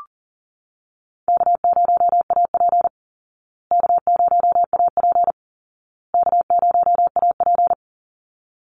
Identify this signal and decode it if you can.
Morse code, heard as a keyed tone, about 40 wpm, 700 Hz. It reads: K0AP K0AP K0AP
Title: Easy Call Signs Repeated 3x - 40wpm